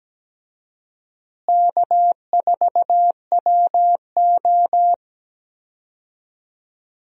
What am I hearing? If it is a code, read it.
Morse code, heard as a keyed tone, about 17 wpm, 700 Hz. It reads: K4WO
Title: USA Call Signs - 4 Characters - Repeated 2x - 17wpm